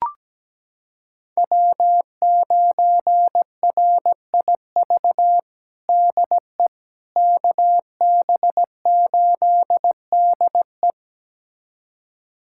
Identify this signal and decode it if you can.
Morse code, heard as a keyed tone, about 17 wpm, 700 Hz. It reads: W9RIV DE KB8DE
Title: USA Callsign DE Callsign - 5 Characters - 17wpm